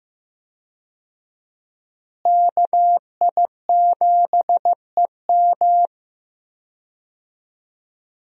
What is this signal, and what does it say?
Morse code, heard as a keyed tone, about 15 wpm, 700 Hz. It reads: KI7EM
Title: USA Call Signs - 5 Characters - Repeated 2x - 15wpm